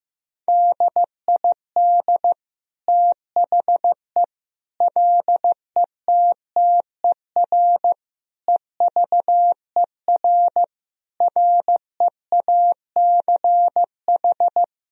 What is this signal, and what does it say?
Morse code, heard as a keyed tone, about 16 wpm, 700 Hz. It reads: DID THE LETTER EVER REACH
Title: Sentences from Top 1000 Words - Review Part 2 - 15wpm